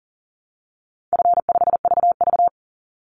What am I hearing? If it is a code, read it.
Morse code, heard as a keyed tone, about 40 wpm, 700 Hz. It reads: F5VV
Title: Easy Call Signs Repeated 3x - 40wpm